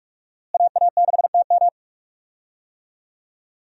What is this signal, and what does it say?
Morse code, heard as a keyed tone, about 43 wpm, 700 Hz. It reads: AA6TM
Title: USA Call Signs - 5 Characters - Repeated 3x - 45wpm